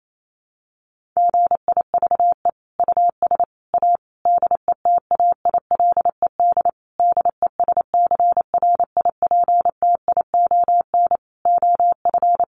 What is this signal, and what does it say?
Morse code, heard as a keyed tone, about 28 wpm, 700 Hz. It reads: GIVE US A DETAILED DESCRIPTION OF
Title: Sentences from Top 1500 Words 28wpm